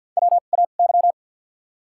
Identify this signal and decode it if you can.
Morse code, heard as a keyed tone, about 53 wpm, spaced as for 37 wpm, 700 Hz. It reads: WAY
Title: 2x Character Spacing - Top 500 Words - 50wpm